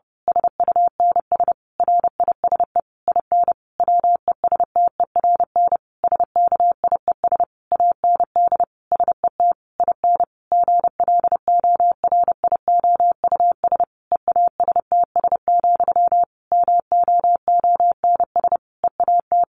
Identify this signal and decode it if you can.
Morse code, heard as a keyed tone, about 30 wpm, 700 Hz. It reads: SUNS RISE IN WESTERN SKIES AND SET IN GLORIOUS EASTS, MOONS EAT
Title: A Sound Of Thunder 30wpm